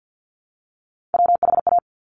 Morse code, heard as a keyed tone, about 50 wpm, 690 Hz. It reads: WHA
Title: Top Three Letter Combinations in Words 50wpm